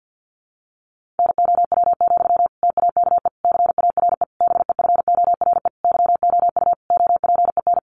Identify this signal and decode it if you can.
Morse code, heard as a keyed tone, about 50 wpm, 700 Hz. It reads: NOW, TAKE CARE BEFORE YOU OPEN
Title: Sentences from Top 1500 Words 50wpm